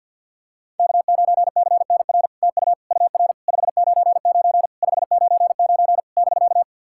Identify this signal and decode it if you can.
Morse code, heard as a keyed tone, about 48 wpm, 700 Hz. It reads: K9CNR TU UR 599 599 <BK>
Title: SOTA QSO Practice 50wpm